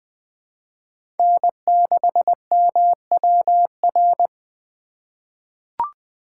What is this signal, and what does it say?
Morse code, heard as a keyed tone, about 21 wpm, 700 Hz. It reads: N6MWR
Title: USA Call Signs - 5 Characters - 2x Character Spacing - 20wpm